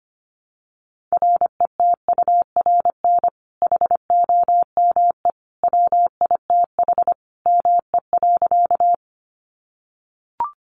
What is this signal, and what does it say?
Morse code, heard as a keyed tone, about 25 wpm, 700 Hz. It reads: RETURN HOME WITH ME.
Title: Sentences from Top 800 Words 25wpm